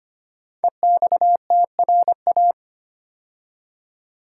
Morse code, heard as a keyed tone, about 25 wpm, 700 Hz. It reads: EXTRA
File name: 5 Letter Words - First 3 Letters Given - 25wpm